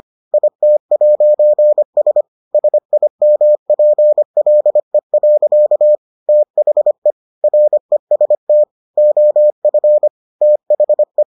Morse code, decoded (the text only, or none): IT'S SIMPLE. THE REST OF THE